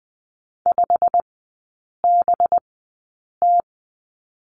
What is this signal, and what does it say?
Morse code, heard as a keyed tone, about 20 wpm, spaced as for 8 wpm, 700 Hz. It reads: 5BT
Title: 184 Lesson 30e - Letter b - Three Characters 20wpm